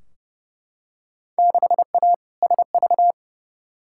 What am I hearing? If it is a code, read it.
Morse code, heard as a keyed tone, about 29 wpm, 700 Hz. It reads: 6A SV
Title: Field Day Exchanges Encore - 30wpm